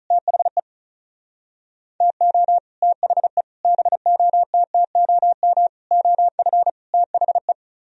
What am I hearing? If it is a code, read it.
Morse code, heard as a keyed tone, about 44 wpm, 700 Hz. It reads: THE TO THE BOTTOM OF THE